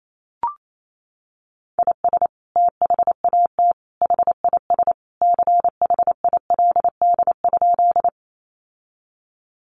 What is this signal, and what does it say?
Morse code, heard as a keyed tone, about 28 wpm, 700 Hz. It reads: IS THAT HIS CHILD?
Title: Sentences from Top 800 Words 28wpm